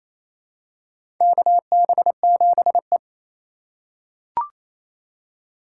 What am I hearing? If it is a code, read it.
Morse code, heard as a keyed tone, about 28 wpm, 700 Hz. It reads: KB7E
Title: USA Call Signs - 4 Characters - 3x Character Spacing - 28wpm